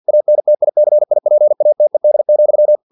Morse code, heard as a keyed tone, about 50 wpm, 600 Hz. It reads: ANTICIPATED,
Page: Never 49wpm